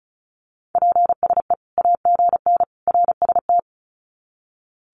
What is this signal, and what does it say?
Morse code, heard as a keyed tone, about 34 wpm, 700 Hz. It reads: PSE AGN RST
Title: Common QSO Phrases 35wpm